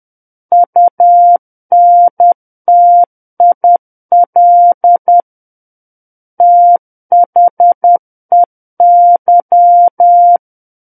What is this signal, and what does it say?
Morse code, heard as a keyed tone, about 10 wpm, 700 Hz. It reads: UNTIL THEY
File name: Book One / 01-Chapter 1 - 10 WPM